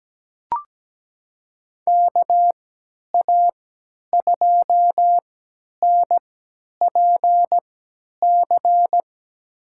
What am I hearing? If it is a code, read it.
Morse code, heard as a keyed tone, about 17 wpm, spaced as for 10 wpm, 700 Hz. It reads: KA2NPC
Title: USA Call Signs - 6 Characters - 3x Character Spacing - 17wpm